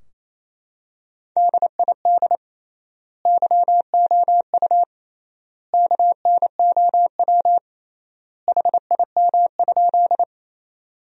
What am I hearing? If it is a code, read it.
Morse code, heard as a keyed tone, about 28 wpm, 700 Hz. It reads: DID YOU KNOW HIM?
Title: 3x Word-Spacing - Sentences from Top 100 Words - 28wpm